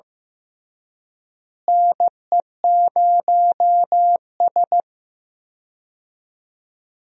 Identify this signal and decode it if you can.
Morse code, heard as a keyed tone, about 16 wpm, 700 Hz. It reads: NE0S
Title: USA Call Signs - 4 Characters - Repeated 3x - 15wpm